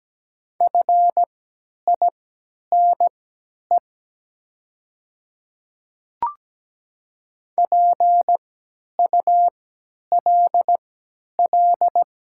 Morse code, heard as a keyed tone, about 17 wpm, spaced as for 10 wpm, 700 Hz. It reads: FINE PULL
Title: 3x Character Spacing - Top 500 Words - 17wpm